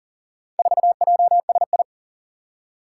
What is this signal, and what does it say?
Morse code, heard as a keyed tone, about 40 wpm, 700 Hz. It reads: 4JSI
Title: Four Challenging Letter-Number Combinations 40wpm